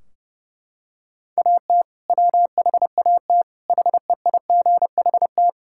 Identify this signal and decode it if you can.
Morse code, heard as a keyed tone, about 30 wpm, 700 Hz. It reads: AT WHAT HEIGHT
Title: Sentences from Top 1600 Words 30wpm